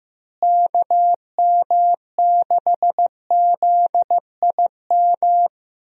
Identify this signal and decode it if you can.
Morse code, heard as a keyed tone, about 15 wpm, 700 Hz. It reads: KM6ZIM
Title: USA Call Signs - 6 Characters - Repeated 2x - 15wpm